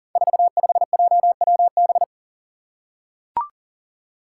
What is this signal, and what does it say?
Morse code, heard as a keyed tone, about 38 wpm, 690 Hz. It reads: 45JWB
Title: Five Challenging Letter-Number Combinations 40wpm